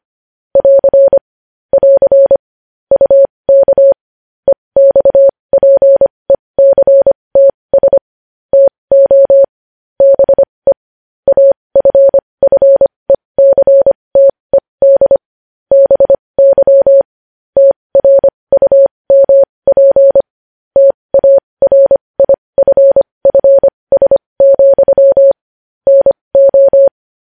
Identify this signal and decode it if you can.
Morse code, heard as a keyed tone, about 25 wpm, spaced as for 20 wpm, 550 Hz. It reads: <AR> <AR> UK EXPECTS TO BE AFFECTED BY TRUMP TARIFFS, NO
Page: News Headlines in Morse Code at 20 WPM